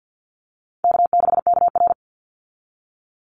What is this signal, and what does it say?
Morse code, heard as a keyed tone, about 50 wpm, 700 Hz. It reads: K6KR
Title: Easy Call Signs Repeated 3x - 50wpm